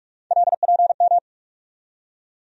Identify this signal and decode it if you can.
Morse code, heard as a keyed tone, about 46 wpm, 700 Hz. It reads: LPM
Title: USA Call Sign Postfixes - 3 Letters 45wpm